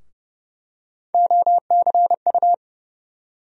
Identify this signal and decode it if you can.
Morse code, heard as a keyed tone, about 30 wpm, 700 Hz. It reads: OCU
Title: USA Call Sign Postfixes - 3 Letters 30wpm